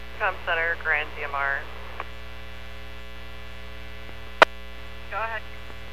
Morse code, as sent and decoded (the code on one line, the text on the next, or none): none